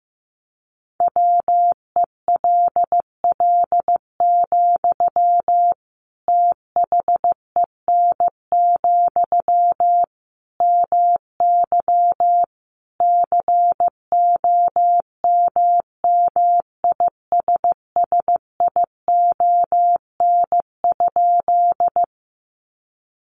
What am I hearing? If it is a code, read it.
Morse code, heard as a keyed tone, about 15 wpm, 700 Hz. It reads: WELL, THEN, MY COMMISSION?
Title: Sentences from Top 1400 Words 15wpm